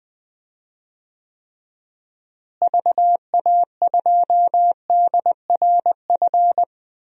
Morse code, heard as a keyed tone, about 20 wpm, 700 Hz. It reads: VA2DRF
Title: DXCC-100 Call Signs - 6 Characters - Repeated 3x - 20wpm